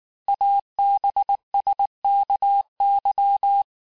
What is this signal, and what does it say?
Morse code, heard as a keyed tone, about 19 wpm, 800 Hz. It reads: ABSKY
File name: cours-du-lundi files / cours-1589-texte-chiffre-vitesse-18mpm